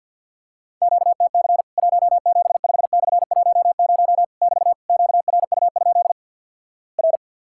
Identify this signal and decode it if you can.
Morse code, heard as a keyed tone, about 50 wpm, 700 Hz. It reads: QTC 175/10 <BT> QRU? R
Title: Worked All Europe DX Contest - QTC Exchange with Repeats - 50wpm